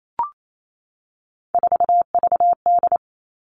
Morse code, heard as a keyed tone, about 28 wpm, 700 Hz. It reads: 4VD
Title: All Cut Numbers - 3-Digit 28wpm